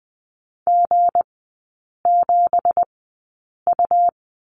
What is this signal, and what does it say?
Morse code, heard as a keyed tone, about 20 wpm, spaced as for 8 wpm, 700 Hz. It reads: G7U
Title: 166 Lesson 27e - Number 7 - Three Characters 20wpm